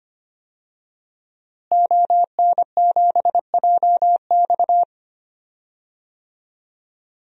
Morse code, heard as a keyed tone, about 25 wpm, 700 Hz. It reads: ON7JX
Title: DXCC-100 Call Signs - 5 Characters - Repeated 3x - 25wpm